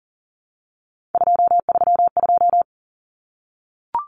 Morse code, heard as a keyed tone, about 40 wpm, 700 Hz. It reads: 232
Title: Numbers - 3-Digit 40wpm